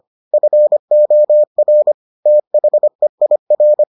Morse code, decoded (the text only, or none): FOR THEIR